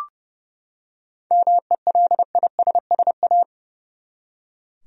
Morse code, heard as a keyed tone, about 31 wpm, 700 Hz. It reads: MELISSA
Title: USA Common Names - 30wpm